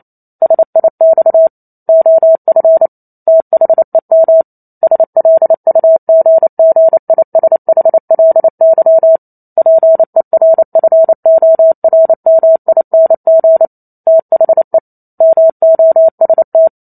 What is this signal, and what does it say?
Morse code, heard as a keyed tone, about 28 wpm, 650 Hz. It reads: SIX OF THEM SLUGGISHLY PERFORMING THE MOST
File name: Chapter 2 - 28 WPM